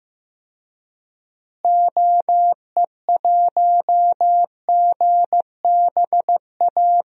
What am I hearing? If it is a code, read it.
Morse code, heard as a keyed tone, about 15 wpm, 700 Hz. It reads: OE1GBA
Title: DXCC-100 Call Signs - 6 Characters - Repeated 3x - 15wpm